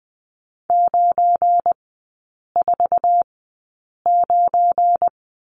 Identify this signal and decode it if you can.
Morse code, heard as a keyed tone, about 20 wpm, spaced as for 8 wpm, 700 Hz. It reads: 949
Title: 172 Lesson 28e - Number 9 - Three Characters 20wpm